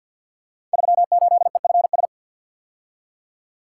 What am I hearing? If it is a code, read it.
Morse code, heard as a keyed tone, about 46 wpm, 700 Hz. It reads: 38EFS
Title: Code Groups - 5 Letter:Numbers - 50wpm